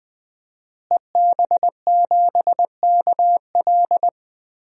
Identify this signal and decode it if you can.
Morse code, heard as a keyed tone, about 20 wpm, 700 Hz. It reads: EB7KL
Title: DXCC-100 Call Signs - 5 Characters - Repeated 2x - 20wpm